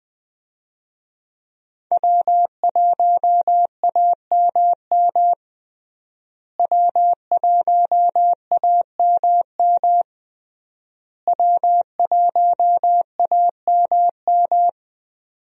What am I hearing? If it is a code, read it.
Morse code, heard as a keyed tone, about 20 wpm, 700 Hz. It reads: W1AMM W1AMM W1AMM
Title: USA Call Signs - 5 Characters - Repeated 3x - 20wpm